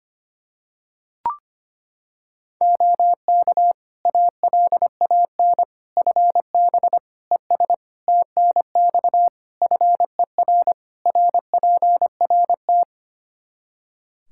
Morse code, heard as a keyed tone, about 25 wpm, 700 Hz. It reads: OK ALAN FB ES TNX FER RPRT
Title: Practice QSO - Exchange 2 - Info - 25wpm